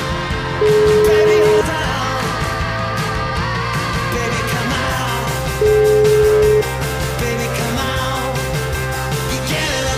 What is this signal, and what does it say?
Ringback tone (Europe, 425 Hz), with noise.